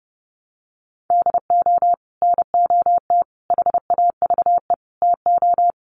Morse code, heard as a keyed tone, about 30 wpm, 700 Hz. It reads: DO NOT HAVE TO
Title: Sets of 5 Words 30wpm